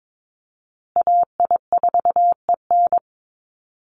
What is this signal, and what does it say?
Morse code, heard as a keyed tone, about 22 wpm, 700 Hz. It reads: AI4EN
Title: USA Call Signs Repeated 2x 22wpm